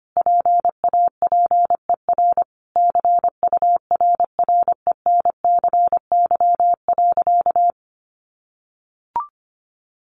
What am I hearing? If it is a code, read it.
Morse code, heard as a keyed tone, about 25 wpm, 700 Hz. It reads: PAPER CURRENCY.
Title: Sentences from Top 1500 Words 25wpm